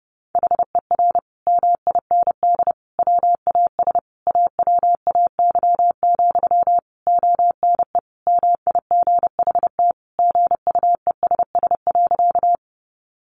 Morse code, heard as a keyed tone, about 30 wpm, 700 Hz. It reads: HER MIND WAS AWAY, ONE MIGHT GUESS.
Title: Sentences from Top 1500 Words 30wpm